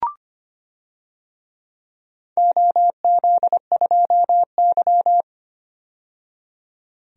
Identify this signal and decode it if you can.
Morse code, heard as a keyed tone, about 25 wpm, 700 Hz. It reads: OZ2Y
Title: DXCC-100 Call Signs - 4 Characters - 25wpm